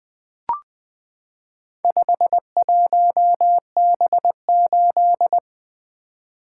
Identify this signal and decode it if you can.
Morse code, heard as a keyed tone, about 20 wpm, 700 Hz. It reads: 51B8
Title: Four Letter-Number Combinations - 20wpm